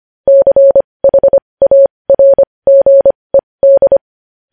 Morse code, heard as a keyed tone, about 25 wpm, spaced as for 20 wpm, 550 Hz. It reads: CHARGED
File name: podcast / News headlines-1x-25x20